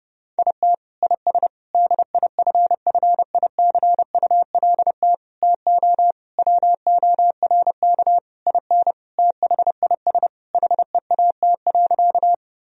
Morse code, heard as a keyed tone, about 30 wpm, 700 Hz. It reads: IT IS DIFFICULT TO WORK IN THIS HEAT.
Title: Sentences from Top 1000 Words - Review Part 3 - 30wpm